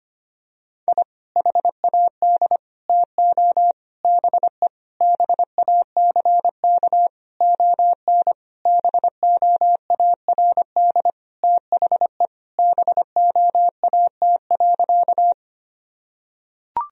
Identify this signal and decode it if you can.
Morse code, heard as a keyed tone, about 25 wpm, 700 Hz. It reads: I HAD TO BE BACK ON BOARD THE BOAT.
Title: Sentences from Top 1000 Words - Review Part 4 - 25wpm